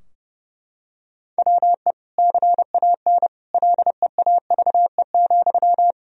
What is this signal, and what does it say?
Morse code, heard as a keyed tone, about 30 wpm, 700 Hz. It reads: WE CAN LEAVE,
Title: Sentences from Top 1000 Words - Review Part 4 - 30wpm